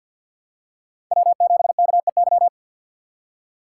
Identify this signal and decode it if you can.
Morse code, heard as a keyed tone, about 52 wpm, 700 Hz. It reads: W7KEY
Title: USA Call Signs - 4x Character Spacing - 50wpm